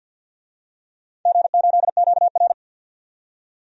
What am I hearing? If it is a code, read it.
Morse code, heard as a keyed tone, about 51 wpm, 700 Hz. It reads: G8QR
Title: DXCC-100 Call Signs - 4 Characters - 50wpm